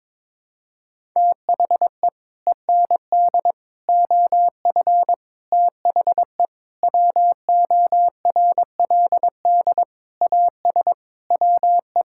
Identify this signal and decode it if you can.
Morse code, heard as a keyed tone, about 22 wpm, 700 Hz. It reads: THE END OF THE WORLD AS WE
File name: Sets of 9 Words - 22wpm